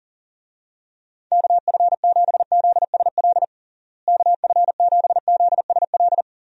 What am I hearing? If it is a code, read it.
Morse code, heard as a keyed tone, about 37 wpm, 700 Hz. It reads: KF7ZSL KF7ZSL
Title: USA Call Signs - 6 Characters - Repeated 2x - 40wpm